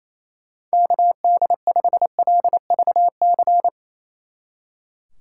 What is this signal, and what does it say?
Morse code, heard as a keyed tone, about 37 wpm, 700 Hz. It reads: KD5LVC